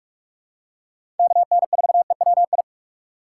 Morse code, heard as a keyed tone, about 45 wpm, 700 Hz. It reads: KN4EWI
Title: K1USN SST QSO Practice - 45wpm